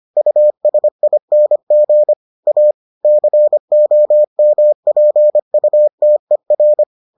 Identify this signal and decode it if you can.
Morse code, heard as a keyed tone, about 25 wpm, 600 Hz. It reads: USING A COMPUTER